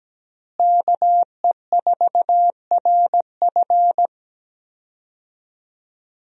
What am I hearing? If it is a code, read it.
Morse code, heard as a keyed tone, about 17 wpm, 700 Hz. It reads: KE4RF
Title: USA Call Signs - 5 Characters - Repeated 3x - 17wpm